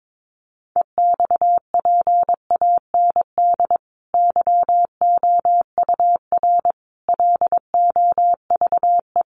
Morse code, heard as a keyed tone, about 22 wpm, 700 Hz. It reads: EXPAND YOUR LOVE
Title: Sentences from Top 1400 Words 22wpm